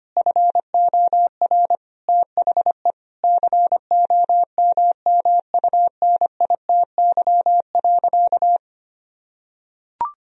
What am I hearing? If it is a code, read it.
Morse code, heard as a keyed tone, about 25 wpm, 700 Hz. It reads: FOR THE COMMUNITY.
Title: Sentences from Top 1000 Words - Review Part 4 - 25wpm